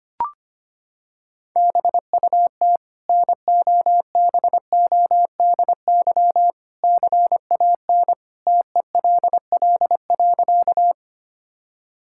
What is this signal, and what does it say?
Morse code, heard as a keyed tone, about 25 wpm, 700 Hz. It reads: BUT NOBODY CAN TELL.
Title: Sentences from Top 1600 Words 25wpm